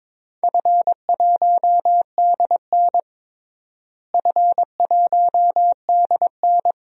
Morse code, heard as a keyed tone, about 22 wpm, 700 Hz. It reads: F1DN F1DN
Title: DXCC-100 Call Signs - 4 Characters - Repeated 2x - 22wpm